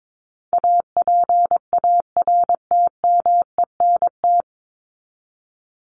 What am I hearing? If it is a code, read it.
Morse code, heard as a keyed tone, about 22 wpm, 700 Hz. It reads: APARTMENT
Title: Top 751-1000 Words 22wpm